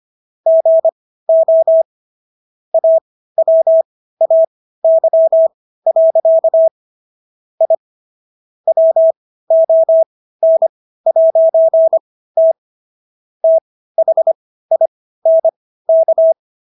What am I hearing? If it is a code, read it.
Morse code, heard as a keyed tone, about 25 wpm, spaced as for 15 wpm, 650 Hz. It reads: GO AWAY. I WON'T THINK